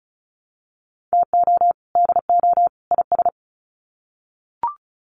Morse code, heard as a keyed tone, about 34 wpm, 700 Hz. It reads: TO DO IS
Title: Sets of 3 Words 35wpm